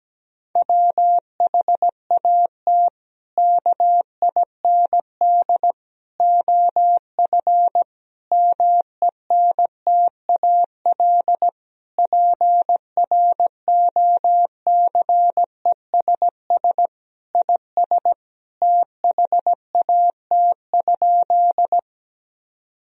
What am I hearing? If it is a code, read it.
Morse code, heard as a keyed tone, about 17 wpm, 700 Hz. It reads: WHAT KIND OF MENTAL PROCESS IS THAT?
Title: Sentences from Top 1000 Words - Review Part 4 - 17wpm